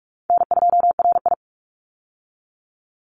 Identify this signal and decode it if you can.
Morse code, heard as a keyed tone, about 44 wpm, 700 Hz. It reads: N2RI
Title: Call Signs - 45wpm